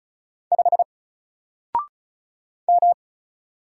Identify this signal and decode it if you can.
Morse code, heard as a keyed tone, about 34 wpm, 700 Hz. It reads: 5 M
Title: Single Letter-Number Alt 35wpm